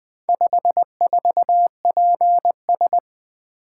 Morse code, heard as a keyed tone, about 20 wpm, 700 Hz. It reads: 54PS
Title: Four Letter-Number Combinations - 20wpm